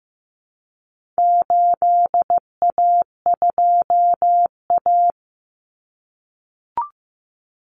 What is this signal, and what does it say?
Morse code, heard as a keyed tone, about 15 wpm, 700 Hz. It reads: 8A2A
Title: Common Cut Numbers - 4-Digit 15wpm